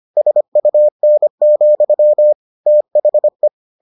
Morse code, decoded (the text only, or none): SUN, THE